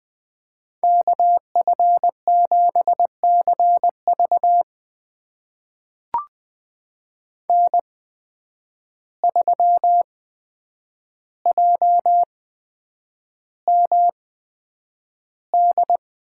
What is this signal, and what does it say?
Morse code, heard as a keyed tone, about 20 wpm, spaced as for 9 wpm, 700 Hz. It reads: KF7CV N3JMD
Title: USA Call Signs - 5 Characters - 8x Character Spacing - 20wpm